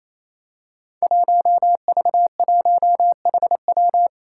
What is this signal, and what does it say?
Morse code, heard as a keyed tone, about 28 wpm, 700 Hz. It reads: 1V1HW
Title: Five Challenging Letter-Number Combinations 28wpm